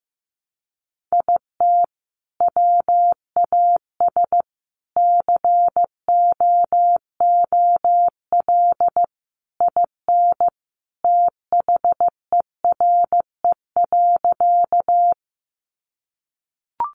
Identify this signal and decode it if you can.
Morse code, heard as a keyed tone, about 14 wpm, 700 Hz. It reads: IT WAS COOL IN THERE.
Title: Sentences from Top 1200 Words 15wpm